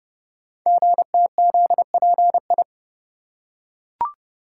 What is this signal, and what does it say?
Morse code, heard as a keyed tone, about 30 wpm, 700 Hz. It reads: GTZPI
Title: Code Groups - 5 Letters - 30wpm